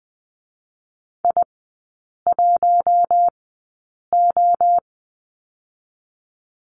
Morse code, heard as a keyed tone, about 20 wpm, spaced as for 8 wpm, 700 Hz. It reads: I1O
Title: 060 Lesson 11e - Letter h - Three Characters 20wpm